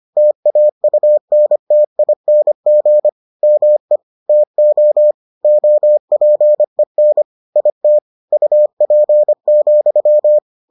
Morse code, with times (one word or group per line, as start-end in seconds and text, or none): 0.2-3.1 TAUNTING
3.4-4.0 ME
4.3-5.1 TO
5.4-7.2 OPEN
7.6-8.0 IT
8.3-10.4 UP,